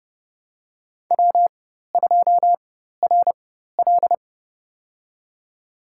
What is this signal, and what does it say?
Morse code, heard as a keyed tone, about 30 wpm, spaced as for 14 wpm, 700 Hz. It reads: W2RL
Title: USA Call Signs - 4 Characters - 4x Character Spacing - 30wpm